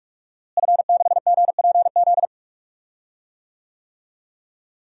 Morse code, heard as a keyed tone, about 42 wpm, 700 Hz. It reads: F6GPZ
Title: DXCC-100 Call Signs - 5 Characters - Repeated 3x - 45wpm